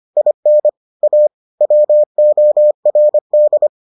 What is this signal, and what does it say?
Morse code, heard as a keyed tone, about 25 wpm, 600 Hz. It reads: IN A WORD